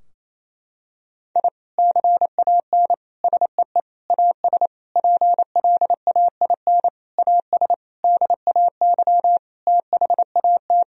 Morse code, heard as a keyed tone, about 28 wpm, 700 Hz. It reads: I CAN SEE AS PLAIN AS DAY THAT
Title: Sentences from Top 1000 Words - Review Part 2 - 28wpm